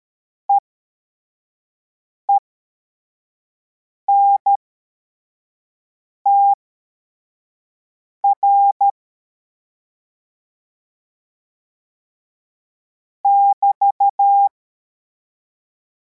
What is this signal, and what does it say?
Morse code, heard as a keyed tone, about 13 wpm, spaced as for 4 wpm, 800 Hz. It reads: EENTR <BT>